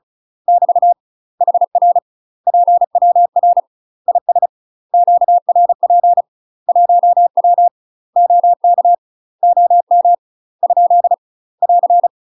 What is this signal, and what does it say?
Morse code, heard as a keyed tone, about 36 wpm, 700 Hz. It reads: <BT> HR PWR IS QRP 1W OK OM ? <AR>
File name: UK MASTER QSOs TRAINING / 35WPM / UK MASTER QSOs 0031